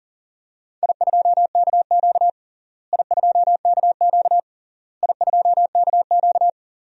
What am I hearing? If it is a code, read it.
Morse code, heard as a keyed tone, about 39 wpm, 700 Hz. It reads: I2KQ I2KQ I2KQ
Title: DXCC-100 Call Signs - 4 Characters - Repeated 3x - 40wpm